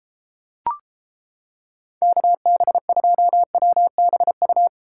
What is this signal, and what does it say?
Morse code, heard as a keyed tone, about 29 wpm, 700 Hz. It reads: KB2WBU